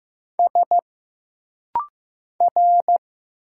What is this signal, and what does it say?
Morse code, heard as a keyed tone, about 15 wpm, 700 Hz. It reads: S R
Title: Single Letters - Rapid-Fire 15wpm